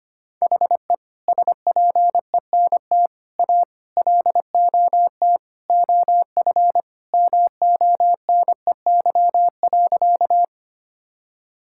Morse code, heard as a keyed tone, about 25 wpm, 700 Hz. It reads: HE SPENT A LOT OF MONEY.
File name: practice / 25wpm / Sentences with 4-Word Phrases - 25wpm